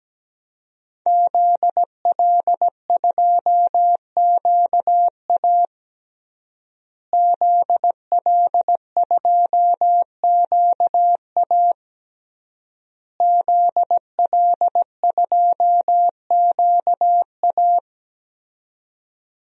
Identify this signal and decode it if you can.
Morse code, heard as a keyed tone, about 17 wpm, 700 Hz. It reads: ZL2QA ZL2QA ZL2QA
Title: DXCC-100 Call Signs - 5 Characters - Repeated 3x - 17wpm